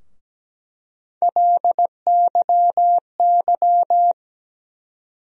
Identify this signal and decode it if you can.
Morse code, heard as a keyed tone, about 17 wpm, 700 Hz. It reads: LYY
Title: USA Call Sign Postfixes - 3 Letters 17wpm